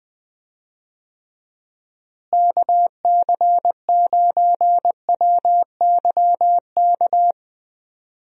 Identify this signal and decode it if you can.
Morse code, heard as a keyed tone, about 20 wpm, 700 Hz. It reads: KC9WYK
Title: USA Call Signs - 6 Characters - Repeated 2x - 20wpm